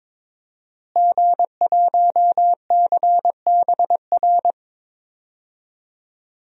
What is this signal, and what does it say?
Morse code, heard as a keyed tone, about 22 wpm, 700 Hz. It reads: G1CBR
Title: DXCC-100 Call Signs - 5 Characters - Repeated 3x - 22wpm